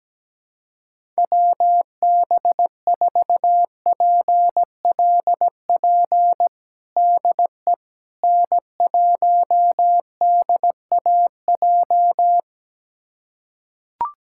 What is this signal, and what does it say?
Morse code, heard as a keyed tone, about 17 wpm, 700 Hz. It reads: WB4PLP DE N1DAJ
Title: USA Callsign DE Callsign 17wpm